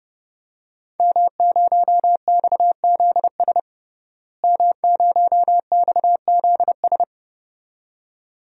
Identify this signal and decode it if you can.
Morse code, heard as a keyed tone, about 31 wpm, 700 Hz. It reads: M0XZS M0XZS
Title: DXCC-100 Call Signs - 5 Characters - Repeated 2x - 30wpm